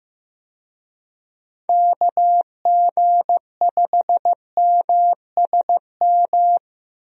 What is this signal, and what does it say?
Morse code, heard as a keyed tone, about 16 wpm, 700 Hz. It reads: KG5MSM
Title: USA Call Signs - 6 Characters - 15wpm